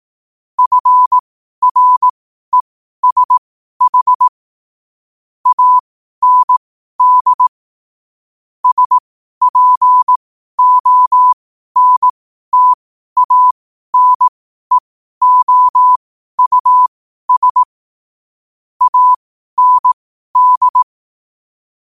Morse code, heard as a keyed tone, about 18 wpm, spaced as for 12 wpm, 1000 Hz. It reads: FRESH AND SPONTANEOUS AND
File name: qotdfiles / 12wpm